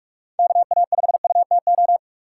Morse code, heard as a keyed tone, about 45 wpm, 700 Hz. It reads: KA5UTO
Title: USA Call Signs - 6 Characters - 6x Character Spacing - 45wpm